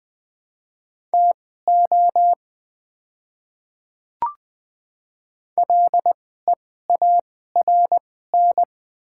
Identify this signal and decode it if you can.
Morse code, heard as a keyed tone, about 20 wpm, spaced as for 14 wpm, 690 Hz. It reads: TO LEARN
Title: 2x Character Spacing - Top 300 Words - 20wpm